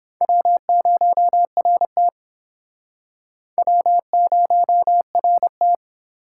Morse code, heard as a keyed tone, about 26 wpm, 700 Hz. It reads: W0RT W0RT
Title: Speed Racing - USA 1x2 Call Signs 22wpm